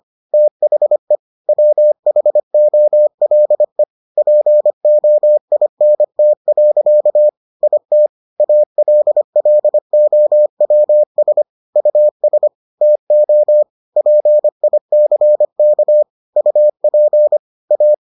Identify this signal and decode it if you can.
Morse code, heard as a keyed tone, about 25 wpm, 600 Hz. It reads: THE WHOLE POINT. IT ALLOWS US TO PICK UP A